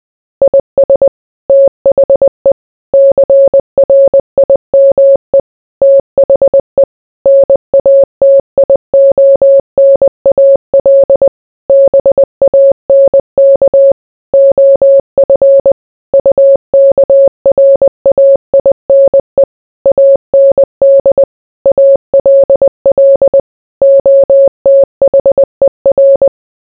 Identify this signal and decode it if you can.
Morse code, heard as a keyed tone, about 20 wpm, 550 Hz. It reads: IS THE CRIME THE NATIONAL BANK OF UKRAINE AND ALL OTHER